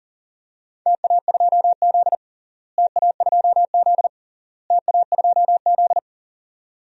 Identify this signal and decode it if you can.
Morse code, heard as a keyed tone, about 40 wpm, 700 Hz. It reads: TA2Z TA2Z TA2Z
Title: DXCC-100 Call Signs - 4 Characters - Repeated 3x - 40wpm